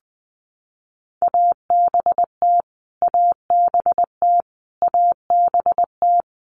Morse code, heard as a keyed tone, about 20 wpm, 700 Hz. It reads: ABT ABT ABT
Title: Abbreviations Repeated 3x - 20wpm